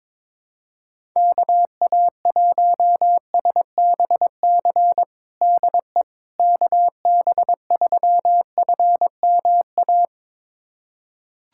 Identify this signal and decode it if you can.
Morse code, heard as a keyed tone, about 22 wpm, 700 Hz. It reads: KA1SBC DE KB3FMA
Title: USA Callsign DE Callsign - 6 Characters - 22wpm